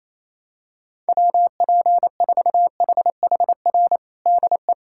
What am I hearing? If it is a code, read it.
Morse code, heard as a keyed tone, about 28 wpm, 700 Hz. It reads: WP4HHR DE
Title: USA Callsign DE Callsign 28wpm